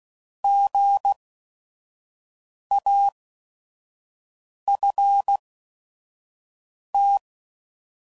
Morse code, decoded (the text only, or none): GAFT